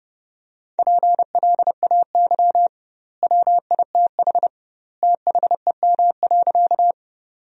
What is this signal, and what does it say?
Morse code, heard as a keyed tone, about 30 wpm, 700 Hz. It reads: PLAY WITH THEM.
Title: 2x Word-Spacing - Sentences from Top 200 Words - 30wpm